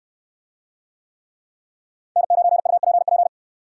Morse code, heard as a keyed tone, about 69 wpm, 700 Hz. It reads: N9UPP